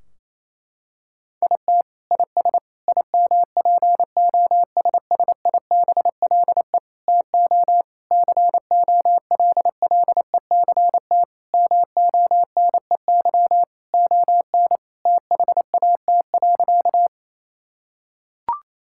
Morse code, heard as a keyed tone, about 28 wpm, 700 Hz. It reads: IT IS IMPOSSIBLE TO COLLECT MONEY ON THAT.
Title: Sentences from Top 1600 Words 28wpm